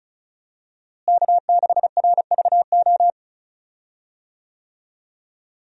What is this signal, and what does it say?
Morse code, heard as a keyed tone, about 36 wpm, 700 Hz. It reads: K6RVO
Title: USA Call Signs - 5 Characters - Repeated 3x - 35wpm